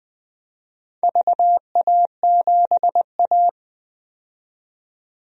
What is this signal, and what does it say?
Morse code, heard as a keyed tone, about 20 wpm, 700 Hz. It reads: VA7A
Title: DXCC-100 Call Signs - 4 Characters - 20wpm